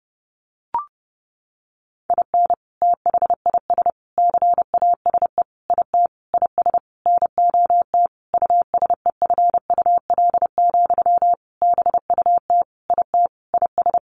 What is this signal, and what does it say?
Morse code, heard as a keyed tone, about 30 wpm, 700 Hz. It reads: IN THIS CASE IT IS NOT USEFUL, BUT IT IS
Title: Sentences from Top 1400 Words 30wpm